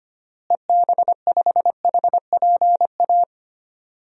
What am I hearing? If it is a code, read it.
Morse code, heard as a keyed tone, about 25 wpm, 700 Hz. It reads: EB5HPA
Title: DXCC-100 Call Signs - 6 Characters - Repeated 3x - 25wpm